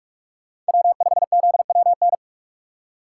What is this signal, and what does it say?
Morse code, heard as a keyed tone, about 58 wpm, 700 Hz. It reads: W5ZWN